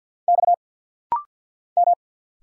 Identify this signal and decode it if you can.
Morse code, heard as a keyed tone, about 50 wpm, 700 Hz. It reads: X M
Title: Single Letters - Rapid-Fire 50wpm